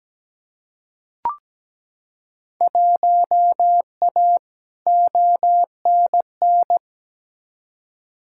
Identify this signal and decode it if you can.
Morse code, heard as a keyed tone, about 17 wpm, 700 Hz. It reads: 1A ONN
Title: Field Day Exchanges Encore - 17wpm